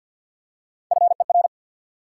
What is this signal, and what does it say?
Morse code, heard as a keyed tone, about 54 wpm, 700 Hz. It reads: FER
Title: Top 180 QSO Elements 50wpm